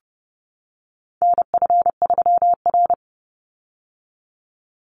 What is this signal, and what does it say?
Morse code, heard as a keyed tone, about 30 wpm, 700 Hz. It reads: NF3R
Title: Call Signs - 30wpm